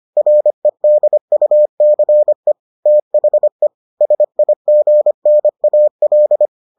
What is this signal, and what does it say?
Morse code, heard as a keyed tone, about 25 wpm, 600 Hz. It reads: REDUCE THE SIGNAL